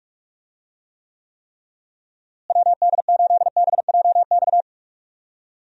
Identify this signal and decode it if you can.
Morse code, heard as a keyed tone, about 43 wpm, 700 Hz. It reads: WD8BJX
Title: USA Call Signs - 6 Characters - Repeated 3x - 45wpm